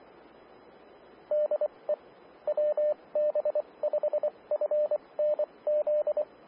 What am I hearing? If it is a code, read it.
Morse code, heard as a keyed tone, about 24 wpm, spaced as for 20 wpm, 600 Hz, with radio noise. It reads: DE WB5FNZ
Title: Austin, TX Amateur Repeaters